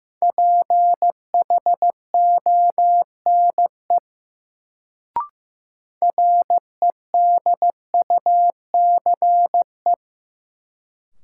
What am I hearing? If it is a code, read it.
Morse code, heard as a keyed tone, about 15 wpm, 700 Hz. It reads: PHONE REDUCE
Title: Sentences from Top 1100 Words - New Word Drill Alt - 15wpm